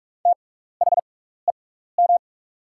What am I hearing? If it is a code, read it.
Morse code, heard as a keyed tone, about 44 wpm, spaced as for 15 wpm, 690 Hz. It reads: THEM
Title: 6x Character Spacing - Top 100 Words - 45wpm